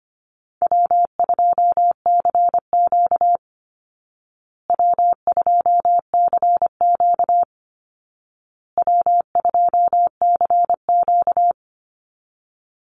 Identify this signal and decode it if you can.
Morse code, heard as a keyed tone, about 25 wpm, 700 Hz. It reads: W2CQ W2CQ W2CQ
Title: Easy Call Signs Repeated 3x - 25wpm